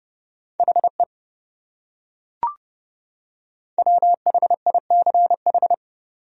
Morse code, heard as a keyed tone, about 29 wpm, 700 Hz. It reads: HE WHICH